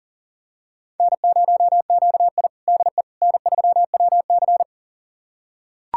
Morse code, heard as a keyed tone, about 40 wpm, 700 Hz. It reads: N0QI DE N3WC
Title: USA Callsign DE Callsign - 4 Characters - 40wpm